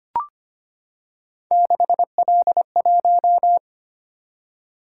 Morse code, heard as a keyed tone, about 25 wpm, 700 Hz. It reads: 6L1
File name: Three Letter-Number Combinations - 25wpm